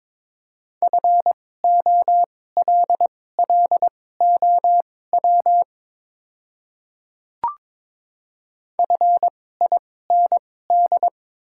Morse code, heard as a keyed tone, about 22 wpm, spaced as for 16 wpm, 700 Hz. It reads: FOLLOW FIND
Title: 2x Character Spacing - Top 200 Words - 22wpm